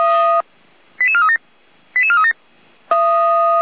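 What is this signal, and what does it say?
Five-tone selective call (ZVEI), heard as quick runs of five tones, tones of about 70 ms. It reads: 88437 88437
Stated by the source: ZVEI five-tone selective call received on an SDR by Feuerwehr Orlamünde.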